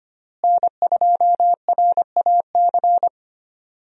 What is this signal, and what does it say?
Morse code, heard as a keyed tone, about 25 wpm, 700 Hz. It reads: N2RAC
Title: USA Call Signs - 5 Characters - Repeated 3x - 25wpm